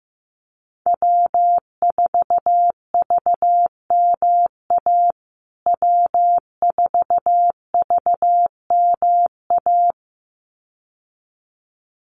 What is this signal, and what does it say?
Morse code, heard as a keyed tone, about 15 wpm, 700 Hz. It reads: W4VMA W4VMA
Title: USA Call Signs Repeated 2x 15wpm